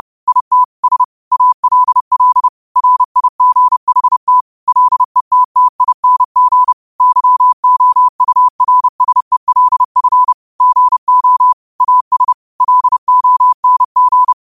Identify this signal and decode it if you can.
Morse code, heard as a keyed tone, about 30 wpm, 1000 Hz. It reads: IT S ALL RIGHT LETTING YOURSELF GO AS LONG
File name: qotdfiles / 30wpm